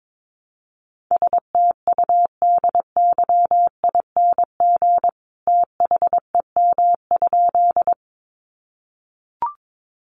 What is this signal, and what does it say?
Morse code, heard as a keyed tone, about 22 wpm, 700 Hz. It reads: STUDYING THEM?
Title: A Sound Of Thunder 22wpm